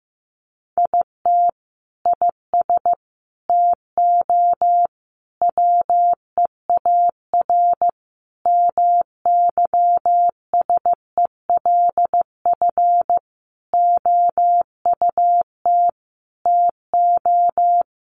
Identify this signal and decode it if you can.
Morse code, heard as a keyed tone, about 15 wpm, 700 Hz. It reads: IT IS TO WEAR MYSELF OUT TO
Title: Sentences from Top 1500 Words 15wpm